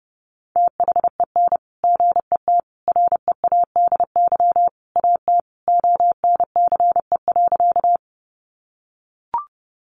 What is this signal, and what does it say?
Morse code, heard as a keyed tone, about 30 wpm, 700 Hz. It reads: THEN GET READY AT ONCE.
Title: Sentences from Top 400 Words 30wpm